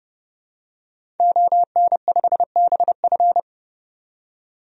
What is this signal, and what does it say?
Morse code, heard as a keyed tone, about 32 wpm, 700 Hz. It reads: ON5BF
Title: DXCC-100 Call Signs - 5 Characters - 30wpm